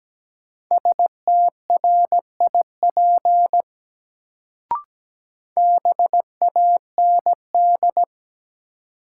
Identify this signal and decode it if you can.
Morse code, heard as a keyed tone, about 17 wpm, 700 Hz. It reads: STRIP BAND
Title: Sentences from Top 1600 Words - New Word Drill - 17wpm